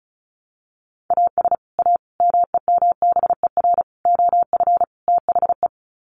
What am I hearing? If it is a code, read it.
Morse code, heard as a keyed tone, about 34 wpm, 700 Hz. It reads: AS A MEMBER OF THE
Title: Sets of 5 Words 35wpm